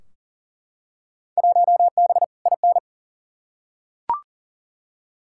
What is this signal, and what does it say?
Morse code, heard as a keyed tone, about 40 wpm, 700 Hz. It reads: 1B IN
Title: Field Day Exchanges - 40wpm